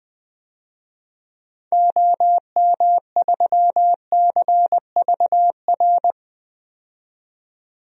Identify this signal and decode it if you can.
Morse code, heard as a keyed tone, about 20 wpm, 700 Hz. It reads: OM3CVR
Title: DXCC-100 Call Signs - 6 Characters - 20wpm